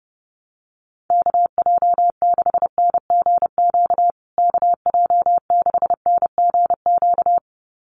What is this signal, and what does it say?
Morse code, heard as a keyed tone, about 30 wpm, 700 Hz. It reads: KJ6NGQ KJ6NGQ
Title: USA Call Signs Repeated 2x 30wpm